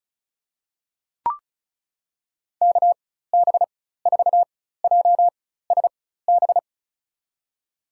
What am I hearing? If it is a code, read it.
Morse code, heard as a keyed tone, about 34 wpm, spaced as for 16 wpm, 700 Hz. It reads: KB4JSB
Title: USA Call Signs - 4x Character Spacing - 35wpm